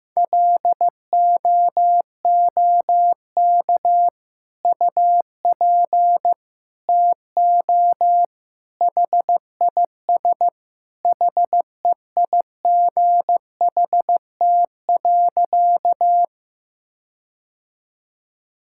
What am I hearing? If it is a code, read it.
Morse code, heard as a keyed tone, about 15 wpm, 700 Hz. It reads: LOOK UP TO HIS HEIGHT.
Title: Sentences from Top 1600 Words 15wpm